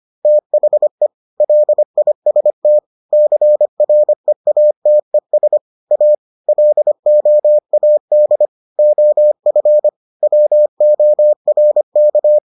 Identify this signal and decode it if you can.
Morse code, heard as a keyed tone, about 25 wpm, 600 Hz. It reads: THE LIST CREATES A LOAD OF WORK